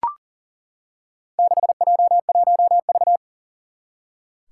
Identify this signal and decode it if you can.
Morse code, heard as a keyed tone, about 40 wpm, 700 Hz. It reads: 6J1V
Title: Four Challenging Letter-Number Combinations 40wpm